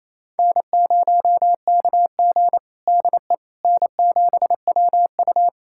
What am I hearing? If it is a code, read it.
Morse code, heard as a keyed tone, about 28 wpm, 700 Hz. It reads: N0KG DE N7WU
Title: USA Callsign DE Callsign - 4 Characters - 28wpm